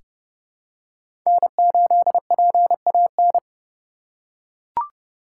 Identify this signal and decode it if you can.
Morse code, heard as a keyed tone, about 30 wpm, 700 Hz. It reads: N8PAN
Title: USA Call Signs - 3x Character Spacing - 30wpm